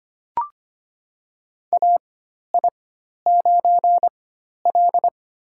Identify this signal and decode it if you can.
Morse code, heard as a keyed tone, about 25 wpm, spaced as for 12 wpm, 700 Hz. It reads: AI9L
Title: USA Call Signs - 4 Characters - 4x Character Spacing - 25wpm